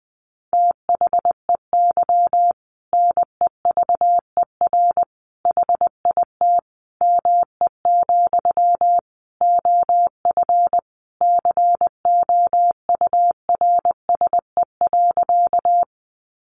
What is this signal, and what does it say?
Morse code, heard as a keyed tone, about 20 wpm, 700 Hz. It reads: THEY NEVER HIT ME, OF COURSE.
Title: Sentences from Top 1200 Words 20wpm